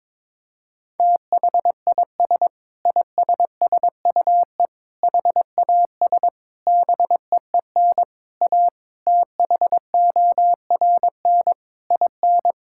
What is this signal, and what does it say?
Morse code, heard as a keyed tone, about 22 wpm, 700 Hz. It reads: THIS ISSUE HAS BEEN A THORN IN
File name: practice / 22wpm / Sentences with 8-Word Phrases - 22wpm